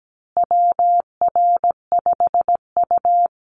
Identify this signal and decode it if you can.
Morse code, heard as a keyed tone, about 17 wpm, 700 Hz. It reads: WR5U
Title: Call Signs Repeated 2x - 17wpm